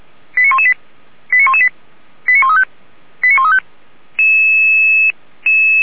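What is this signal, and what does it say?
Five-tone selective call (ZVEI), heard as quick runs of five tones, tones of about 70 ms. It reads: 89118 89118 89135 89135
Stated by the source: Received by SDR.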